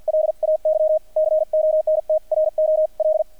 Morse code, heard as a keyed tone, about 50 wpm, spaced as for 43 wpm, 620 Hz, with radio noise. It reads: LAY CONTROL
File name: archived / CRSnet